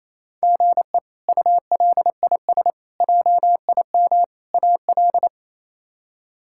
Morse code, heard as a keyed tone, about 28 wpm, 700 Hz. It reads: GE ULIS JIM AL
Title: K1USN SST QSO Practice - 28wpm